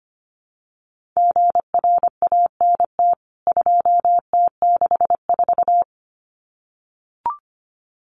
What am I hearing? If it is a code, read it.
Morse code, heard as a keyed tone, about 25 wpm, 700 Hz. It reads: GRANT 2T64
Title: CWT Exchange 25wpm